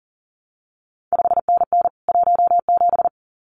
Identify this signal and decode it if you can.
Morse code, heard as a keyed tone, about 41 wpm, 700 Hz. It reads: 5NN 17
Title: CQ WW DX Contest Exchange Encore - 40wpm